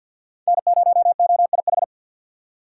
Morse code, heard as a keyed tone, about 49 wpm, 700 Hz. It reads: N0OIH
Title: USA Call Signs - 5 Characters - 6x Character Spacing - 50wpm